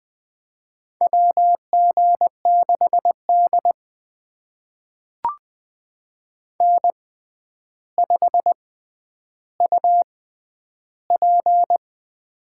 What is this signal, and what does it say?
Morse code, heard as a keyed tone, about 20 wpm, spaced as for 10 wpm, 700 Hz. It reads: WG6D N5UP
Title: USA Call Signs - 4 Characters - 6x Character Spacing - 20wpm